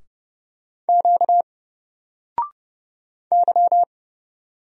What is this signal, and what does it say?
Morse code, heard as a keyed tone, about 30 wpm, 700 Hz. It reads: Q Y
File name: Single-Letter-Number-30wpm